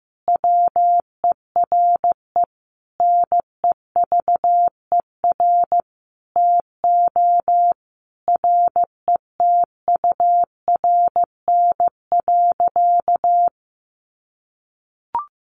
Morse code, heard as a keyed tone, about 15 wpm, 700 Hz. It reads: WERE NEVER TO RETURN.
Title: Sentences from Top 800 Words 15wpm